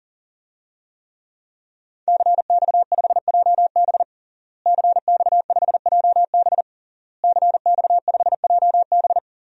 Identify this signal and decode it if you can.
Morse code, heard as a keyed tone, about 40 wpm, 700 Hz. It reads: CX5JB CX5JB CX5JB
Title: DXCC-100 Call Signs - 5 Characters - Repeated 3x - 40wpm